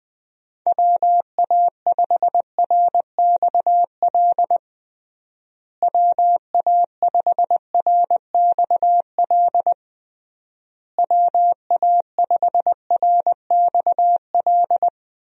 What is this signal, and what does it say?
Morse code, heard as a keyed tone, about 20 wpm, 700 Hz. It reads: WA5RXL WA5RXL WA5RXL
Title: USA Call Signs - 6 Characters - Repeated 3x - 20wpm